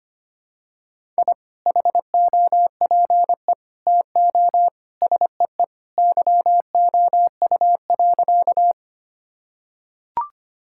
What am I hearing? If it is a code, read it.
Morse code, heard as a keyed tone, about 25 wpm, 700 Hz. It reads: I HOPE TO SEE YOU.
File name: practice / 25wpm / Sentences with 2-Word Phrases - 25wpm